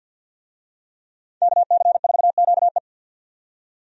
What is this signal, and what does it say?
Morse code, heard as a keyed tone, about 51 wpm, 700 Hz. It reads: KC4QE
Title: USA Call Signs - 5 Characters - 50wpm